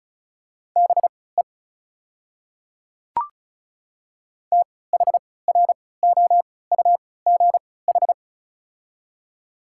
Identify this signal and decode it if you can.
Morse code, heard as a keyed tone, about 36 wpm, spaced as for 20 wpm, 690 Hz. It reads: BE THROUGH
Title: 3x Character Spacing - Top 200 Words - 35wpm